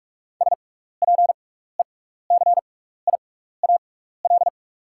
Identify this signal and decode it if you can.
Morse code, heard as a keyed tone, about 44 wpm, spaced as for 15 wpm, 690 Hz. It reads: SPECIAL
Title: 6x Character Spacing - Top 500 Words - 45wpm